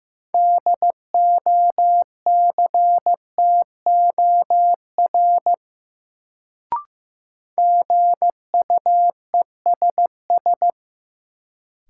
Morse code, heard as a keyed tone, about 15 wpm, 700 Hz. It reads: DOCTOR GUESS
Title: Sentences from Top 1500 Words - New Word Drill - 15wpm